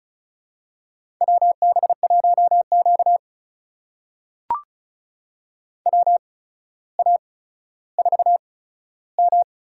Morse code, heard as a keyed tone, about 36 wpm, spaced as for 15 wpm, 700 Hz. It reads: WB1Q WA4M
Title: USA Call Signs - 8x Character Spacing - 35wpm